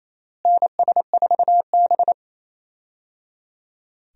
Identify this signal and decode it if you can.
Morse code, heard as a keyed tone, about 27 wpm, 700 Hz. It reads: NS4B
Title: USA Call Signs - 4 Characters - Repeated 3x - 28wpm